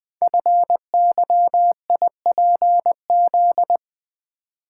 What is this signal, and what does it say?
Morse code, heard as a keyed tone, about 20 wpm, 700 Hz. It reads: FYIPZ
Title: Code Groups - 5 Letter:Numbers - 20wpm